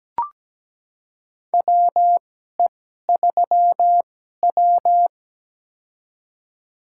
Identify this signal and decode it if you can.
Morse code, heard as a keyed tone, about 17 wpm, spaced as for 12 wpm, 700 Hz. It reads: WE3W
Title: USA Call Signs - 4 Characters - 2x Character Spacing - 17wpm